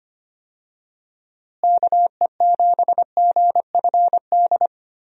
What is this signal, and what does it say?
Morse code, heard as a keyed tone, about 25 wpm, 700 Hz. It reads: KE7GFD
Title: USA Call Signs - 6 Characters - Repeated 3x - 25wpm